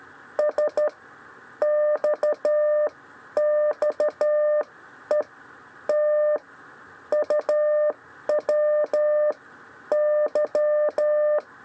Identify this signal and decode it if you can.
Morse code, heard as a keyed tone, about 13 wpm, 600 Hz. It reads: S X X E T U W Y
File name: 6 lezione cw 1